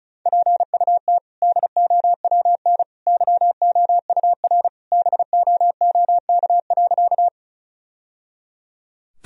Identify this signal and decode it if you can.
Morse code, heard as a keyed tone, about 36 wpm, 700 Hz. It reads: PUT DOWN YOUR BOOK.
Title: Sentences from Top 1000 Words - Review Part 1 - 35wpm